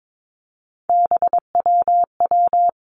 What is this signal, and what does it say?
Morse code, heard as a keyed tone, about 22 wpm, 700 Hz. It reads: BWW
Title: Three Challenging Letter-Number Combinations 22wpm